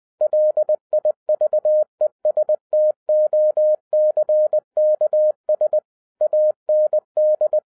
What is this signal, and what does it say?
Morse code, heard as a keyed tone, about 20 wpm, 610 Hz. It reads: LIVESTOCKS AND